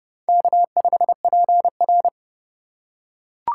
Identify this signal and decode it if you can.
Morse code, heard as a keyed tone, about 30 wpm, 700 Hz. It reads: K5PR
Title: USA Call Signs - 4 Characters - 3x Character Spacing - 30wpm